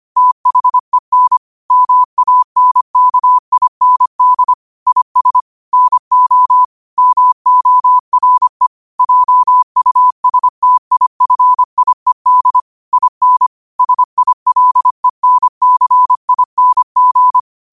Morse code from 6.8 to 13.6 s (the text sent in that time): MORE JUSTIFIED IN